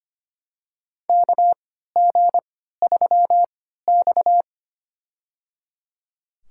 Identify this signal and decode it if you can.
Morse code, heard as a keyed tone, about 25 wpm, spaced as for 14 wpm, 700 Hz. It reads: KG3X
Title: USA Call Signs - 4 Characters - 3x Character Spacing - 25wpm